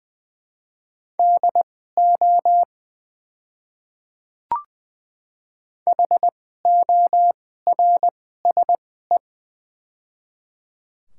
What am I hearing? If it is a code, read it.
Morse code, heard as a keyed tone, about 20 wpm, spaced as for 15 wpm, 700 Hz. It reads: DO HORSE
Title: 2x Character Spacing - Top 300 Words - 20wpm